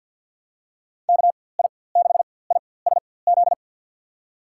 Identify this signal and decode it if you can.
Morse code, heard as a keyed tone, about 53 wpm, spaced as for 24 wpm, 700 Hz. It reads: KI6ISZ
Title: USA Call Signs - 6 Characters - 4x Character Spacing - 50wpm